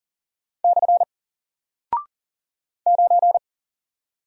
Morse code, heard as a keyed tone, about 40 wpm, 700 Hz. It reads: / 9
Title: Single Character (Letter:Number:Punctuation:Prosign) 40wpm